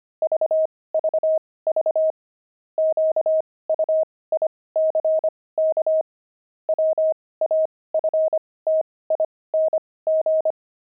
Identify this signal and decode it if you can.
Morse code, heard as a keyed tone, about 25 wpm, spaced as for 18 wpm, 630 Hz. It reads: VVV QUICK WAFTING
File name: pangram-fs-18-cs-25